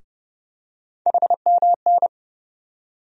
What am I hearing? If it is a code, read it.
Morse code, heard as a keyed tone, about 30 wpm, 700 Hz. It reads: HMN
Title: USA Call Sign Postfixes - 3 Letters 30wpm